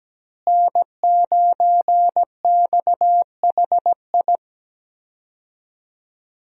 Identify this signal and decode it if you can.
Morse code, heard as a keyed tone, about 17 wpm, 700 Hz. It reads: N9XHI
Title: USA Call Signs - 5 Characters - Repeated 3x - 17wpm